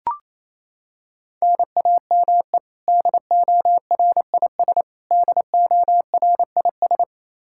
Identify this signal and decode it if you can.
Morse code, heard as a keyed tone, about 28 wpm, 700 Hz. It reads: NAME DORIS DORIS
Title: Practice QSO - Exchange 1 - Beginning - 28wpm